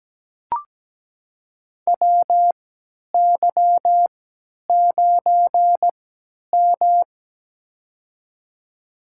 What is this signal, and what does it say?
Morse code, heard as a keyed tone, about 17 wpm, spaced as for 10 wpm, 700 Hz. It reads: WY9M
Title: USA Call Signs - 3x Character Spacing - 17wpm